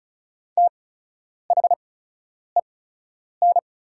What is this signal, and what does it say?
Morse code, heard as a keyed tone, about 36 wpm, spaced as for 10 wpm, 700 Hz. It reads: THEN
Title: 8x Character Spacing - Top 200 Words - 35wpm